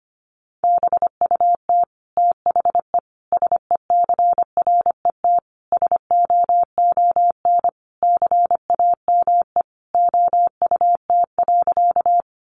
Morse code, heard as a keyed tone, about 25 wpm, 700 Hz. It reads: BUT THE SECRET SOON CAME OUT.
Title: Sentences from Top 1200 Words 25wpm